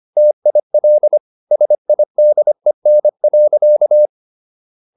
Morse code, heard as a keyed tone, about 25 wpm, 600 Hz. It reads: TIL SIDEN.
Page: Ferie 25 wpm